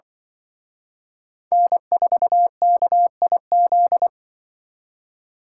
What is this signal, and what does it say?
Morse code, heard as a keyed tone, about 23 wpm, 700 Hz. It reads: N4KIZ
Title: Speed Racing - USA 1x3 Call Signs 20wpm